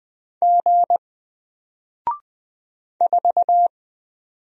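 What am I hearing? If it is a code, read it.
Morse code, heard as a keyed tone, about 20 wpm, 700 Hz. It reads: G 4
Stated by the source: Single letter and number Morse drill at 20 wpm, from N4LCD.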